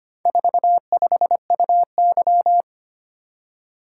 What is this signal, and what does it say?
Morse code, heard as a keyed tone, about 25 wpm, 700 Hz. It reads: 45UY
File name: practice / 25wpm / Four Letter-Number Combinations - 25wpm